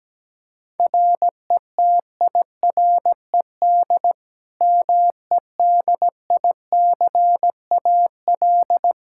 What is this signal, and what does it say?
Morse code, heard as a keyed tone, about 17 wpm, 700 Hz. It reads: RETIRED MEDICAL
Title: Practice QSO - Exchange 3 - More Info - 17wpm